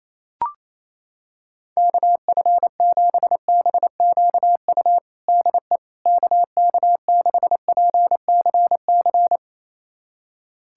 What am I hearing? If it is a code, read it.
Morse code, heard as a keyed tone, about 28 wpm, 700 Hz. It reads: KF7BQU DE KK6PCC
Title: USA Callsign DE Callsign 28wpm